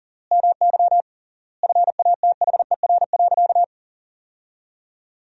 Morse code, heard as a keyed tone, about 40 wpm, 700 Hz. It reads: MY FATHER.
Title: 3x Word-Spacing - Sentences from Top 400 Words - 40wpm